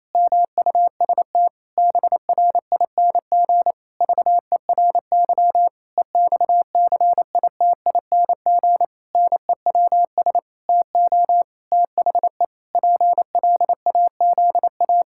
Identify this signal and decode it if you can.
Morse code, heard as a keyed tone, about 28 wpm, 700 Hz. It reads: MUST BRING VERY EXCITING NEWS TO THE PLAZA
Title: Pangram Challenge v2 - 28wpm